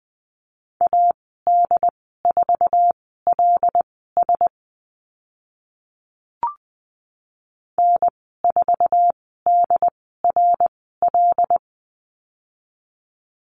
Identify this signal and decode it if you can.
Morse code, heard as a keyed tone, about 20 wpm, spaced as for 14 wpm, 700 Hz. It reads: AD4LS N4DRL
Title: 147 Lesson 24g - Letter y - Call Signs 20wpm